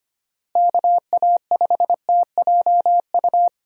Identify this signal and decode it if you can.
Morse code, heard as a keyed tone, about 25 wpm, 700 Hz. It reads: KA5TJU
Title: USA Call Signs - 6 Characters - Repeated 2x - 25wpm